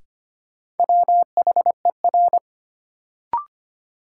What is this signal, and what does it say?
Morse code, heard as a keyed tone, about 25 wpm, 700 Hz. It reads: WHER
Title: Sentences from Top 1500 Words - New Word Drill - 25wpm